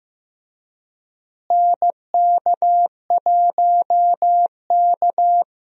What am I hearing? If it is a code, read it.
Morse code, heard as a keyed tone, about 15 wpm, 700 Hz. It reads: NK1K
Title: USA Call Signs - 4 Characters - Repeated 2x - 15wpm